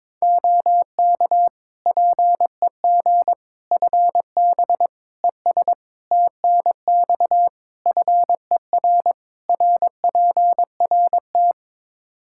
Morse code, heard as a keyed tone, about 21 wpm, 700 Hz. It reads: OK PEG FB ES TNX FER RPRT
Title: Practice QSO - Exchange 2 - Info - 22wpm